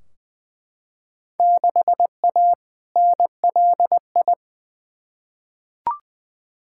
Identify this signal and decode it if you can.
Morse code, heard as a keyed tone, about 20 wpm, 700 Hz. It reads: 6A NLI
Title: Field Day Exchanges Encore - 20wpm